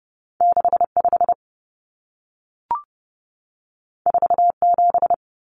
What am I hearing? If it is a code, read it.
Morse code, heard as a keyed tone, about 30 wpm, 700 Hz. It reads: 65 47
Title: Numbers - 2-Digit 30wpm